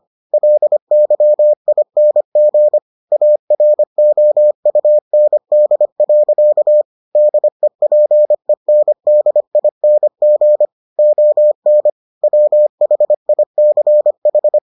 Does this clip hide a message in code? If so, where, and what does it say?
Yes: Morse code from 0.3 to 14.6 s, heard as a keyed tone, about 25 wpm, 600 Hz. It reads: LYING AROUND. DEPENDING ON WHICH